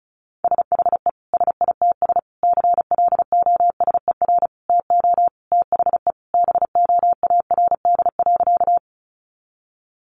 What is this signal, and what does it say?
Morse code, heard as a keyed tone, about 34 wpm, 700 Hz. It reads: SHE SITS CLOSER TO THE BOARD.
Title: Additional Easy Phrases 35wpm